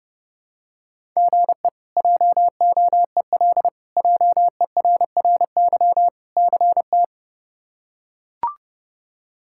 Morse code, heard as a keyed tone, about 30 wpm, 700 Hz. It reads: GE JOEL JERRY CT
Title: K1USN SST QSO Practice - 30wpm